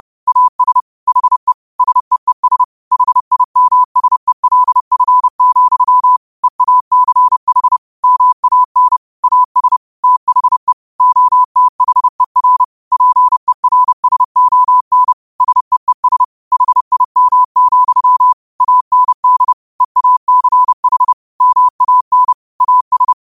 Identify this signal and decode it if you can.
Morse code, heard as a keyed tone, about 30 wpm, 1000 Hz. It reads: AS HE SEES HIMSELF, EACH MAN AS THE OTHER PERSON SEES HIM, AND EACH MAN AS